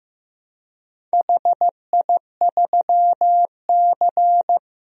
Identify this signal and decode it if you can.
Morse code, heard as a keyed tone, about 15 wpm, 700 Hz. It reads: HI3C
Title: DXCC-100 Call Signs - 4 Characters - Repeated 2x - 15wpm